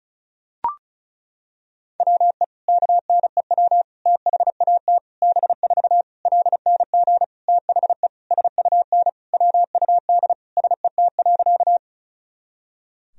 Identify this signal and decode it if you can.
Morse code, heard as a keyed tone, about 36 wpm, 700 Hz. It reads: WE KNEW THAT B4 LNG THE SUN WUD SET.
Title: Sentences from Top 400 Words and Abbreviations 35wpm